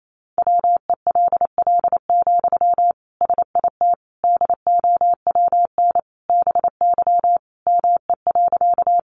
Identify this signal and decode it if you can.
Morse code, heard as a keyed tone, about 28 wpm, 700 Hz. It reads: WELL, SIT DOWN BY ME.
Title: Sentences from Top 500 Words 28wpm